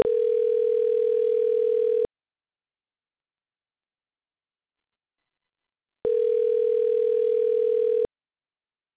Ringback tone (North America), 440 + 480 Hz.